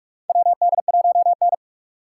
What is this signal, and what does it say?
Morse code, heard as a keyed tone, about 46 wpm, 700 Hz. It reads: WD1N
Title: USA Call Signs - 4 Characters - 4x Character Spacing - 45wpm